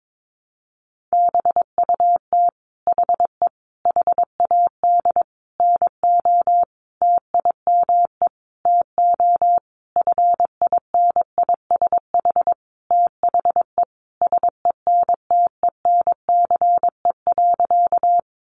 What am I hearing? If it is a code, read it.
Morse code, heard as a keyed tone, about 22 wpm, 700 Hz. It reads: BUT HE HAD NO TIME TO FINISH THE SENTENCE.
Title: Sentences from Top 1200 Words 22wpm